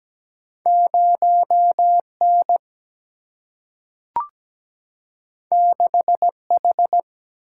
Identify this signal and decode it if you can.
Morse code, heard as a keyed tone, about 17 wpm, 700 Hz. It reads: 0N 6H
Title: Single-Letter Single-Number 17wpm